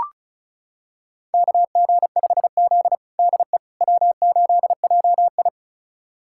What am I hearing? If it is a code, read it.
Morse code, heard as a keyed tone, about 36 wpm, 700 Hz. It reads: KG5Z DE W8JI
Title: USA Callsign DE Callsign - 4 Characters - 35wpm